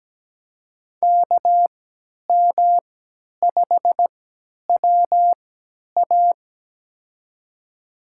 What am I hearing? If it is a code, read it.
Morse code, heard as a keyed tone, about 17 wpm, spaced as for 10 wpm, 700 Hz. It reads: KM5WA
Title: USA Call Signs - 5 Characters - 3x Character Spacing - 17wpm